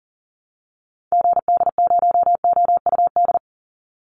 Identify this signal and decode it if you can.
Morse code, heard as a keyed tone, about 40 wpm, 700 Hz. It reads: GD0OUD
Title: Call Signs Repeated 3x - 40wpm